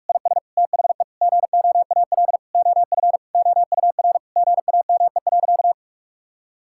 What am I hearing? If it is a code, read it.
Morse code, heard as a keyed tone, about 47 wpm, 700 Hz. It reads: IS THE GOAL OF OUR GAME.
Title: Sentences from Top 1000 Words - Review Part 5 - 45wpm